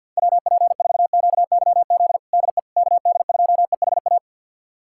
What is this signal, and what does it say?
Morse code, heard as a keyed tone, about 53 wpm, 700 Hz. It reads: WP4QYZ DE KD2EHA
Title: USA Callsign DE Callsign - 6 Characters - 50wpm